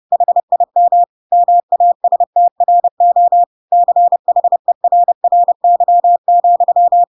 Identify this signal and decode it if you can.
Morse code, heard as a keyed tone, about 30 wpm, 700 Hz. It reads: HIM MASTRO CHERRY,
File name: Pinocchio / Pinocchio ch1 30 30 ik1hge